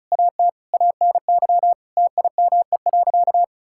Morse code, heard as a keyed tone, about 36 wpm, 700 Hz. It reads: AT ANY TIME.
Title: Sentences from Top 1000 Words - Review Part 6 - 35wpm